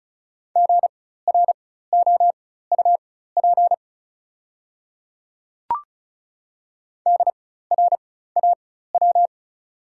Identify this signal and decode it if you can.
Morse code, heard as a keyed tone, about 36 wpm, spaced as for 16 wpm, 700 Hz. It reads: GROUP DRAW
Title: 4x Character Spacing - Top 300 Words - 35wpm